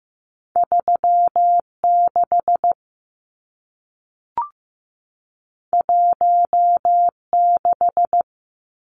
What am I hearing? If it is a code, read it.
Morse code, heard as a keyed tone, about 15 wpm, 700 Hz. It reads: 36 16
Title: Numbers - 2-Digit 15wpm